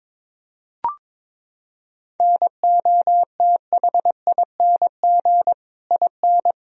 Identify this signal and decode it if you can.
Morse code, heard as a keyed tone, about 22 wpm, 700 Hz. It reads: NOTHING IN
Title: Sentences from Top 1000 Words - Review Part 4 - 22wpm